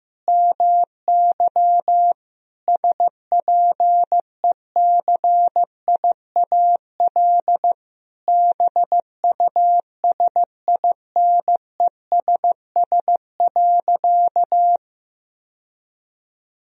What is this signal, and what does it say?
Morse code, heard as a keyed tone, about 16 wpm, 700 Hz. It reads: MY SPECIAL BUSINESS.
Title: Sentences from Top 1000 Words - Review Part 3 - 15wpm